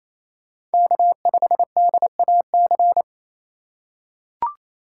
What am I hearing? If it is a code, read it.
Morse code, heard as a keyed tone, about 28 wpm, 700 Hz. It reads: K5DAC
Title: USA Call Signs - 5 Characters - 2x Character Spacing - 28wpm